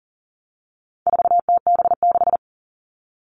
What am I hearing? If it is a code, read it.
Morse code, heard as a keyed tone, about 40 wpm, 700 Hz. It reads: 4TB6
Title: All Cut Numbers - 4-Digit 40wpm